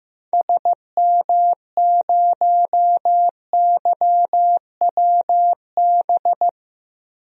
Morse code, heard as a keyed tone, about 15 wpm, 700 Hz. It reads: SM0YWB
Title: DXCC-100 Call Signs - 6 Characters - 15wpm